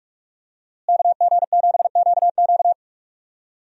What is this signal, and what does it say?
Morse code, heard as a keyed tone, about 41 wpm, 700 Hz. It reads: KG7QQ
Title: USA Call Signs - 5 Characters - Repeated 2x - 45wpm